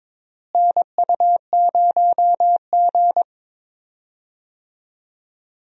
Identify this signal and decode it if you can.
Morse code, heard as a keyed tone, about 21 wpm, 700 Hz. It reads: NU0G
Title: USA Call Signs - 4 Characters - Repeated 2x - 22wpm